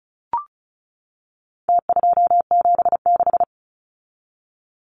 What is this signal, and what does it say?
Morse code, heard as a keyed tone, about 34 wpm, 700 Hz. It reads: T276
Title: Numbers - Contest Style - 4-Digit 35wpm